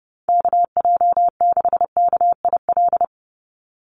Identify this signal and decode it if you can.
Morse code, heard as a keyed tone, about 30 wpm, 700 Hz. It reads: KJ6KIL
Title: USA Call Signs 30wpm